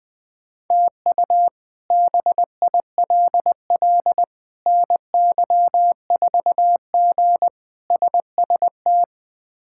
Morse code, heard as a keyed tone, about 20 wpm, 700 Hz. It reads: TU BILL NY4G SST
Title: K1USN SST QSO Practice - 20wpm